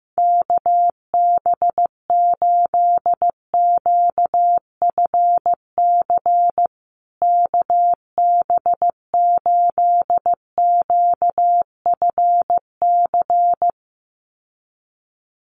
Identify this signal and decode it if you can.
Morse code, heard as a keyed tone, about 15 wpm, 700 Hz. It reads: KB8QFC KB8QFC
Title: USA Call Signs Repeated 2x 15wpm